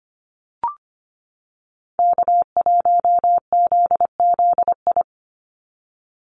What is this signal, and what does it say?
Morse code, heard as a keyed tone, about 25 wpm, 700 Hz. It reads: K1ZZI
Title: Call Signs - 25wpm